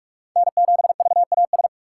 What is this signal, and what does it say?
Morse code, heard as a keyed tone, about 46 wpm, 700 Hz. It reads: N7VAS
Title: USA Call Signs - 5 Characters - 3x Character Spacing - 45wpm